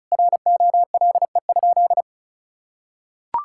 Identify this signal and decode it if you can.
Morse code, heard as a keyed tone, about 36 wpm, 700 Hz. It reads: ROLE?
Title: Sentences from Top 1000 Words - Review Part 6 - 35wpm